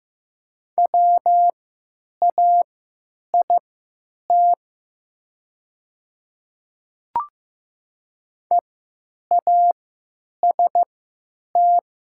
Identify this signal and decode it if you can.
Morse code, heard as a keyed tone, about 15 wpm, spaced as for 8 wpm, 700 Hz. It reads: WAIT EAST
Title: 3x Character Spacing - Top 500 Words - 15wpm